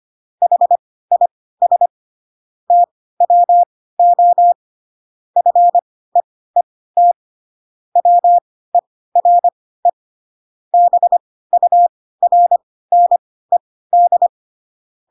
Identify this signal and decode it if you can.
Morse code, heard as a keyed tone, about 25 wpm, spaced as for 16 wpm, 700 Hz. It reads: HIS TWO FEET WERE BURNED